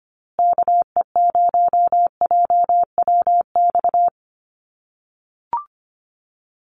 Morse code, heard as a keyed tone, about 25 wpm, 700 Hz. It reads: KE0JWX
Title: USA Call Signs Repeated 2x 25wpm